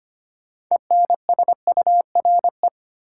Morse code, heard as a keyed tone, about 25 wpm, 700 Hz. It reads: ENSURE
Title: Sentences from Top 1100 Words - New Word Drill - 25wpm